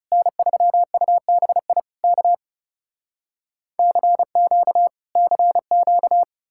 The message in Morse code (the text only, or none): N3UBI K CQ CQ